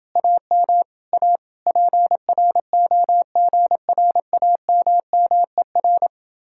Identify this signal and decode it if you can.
Morse code, heard as a keyed tone, about 30 wpm, 700 Hz. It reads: AM A PROGRAMMER